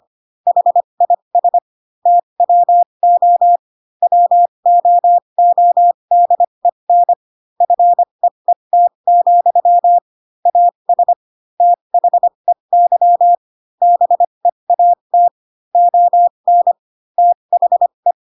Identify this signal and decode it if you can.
Morse code, heard as a keyed tone, about 25 wpm, spaced as for 22 wpm, 700 Hz. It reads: HIS TWO WOODEN FEET, AS THEY BEAT ON THE
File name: Pinocchio / Pinocchio ch3 18 25 ik1hge